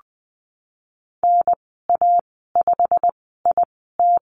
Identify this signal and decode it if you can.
Morse code, heard as a keyed tone, about 20 wpm, spaced as for 14 wpm, 700 Hz. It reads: NA5IT
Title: 088 Lesson 15g - Number 5 - Call Signs 20wpm